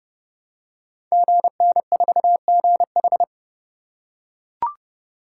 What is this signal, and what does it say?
Morse code, heard as a keyed tone, about 30 wpm, 700 Hz. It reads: GN4GH
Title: Code Groups - 5 Letter:Numbers - 30wpm